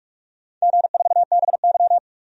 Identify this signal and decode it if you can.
Morse code, heard as a keyed tone, about 47 wpm, 700 Hz. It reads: G4BY
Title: DXCC-100 Call Signs - 4 Characters - 45wpm